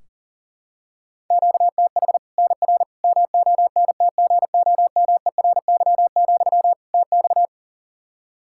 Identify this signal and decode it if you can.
Morse code, heard as a keyed tone, about 37 wpm, 700 Hz. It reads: QTH NR MONTGOMERY, TX
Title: Practice QSO - Exchange 1 - Beginning - 40wpm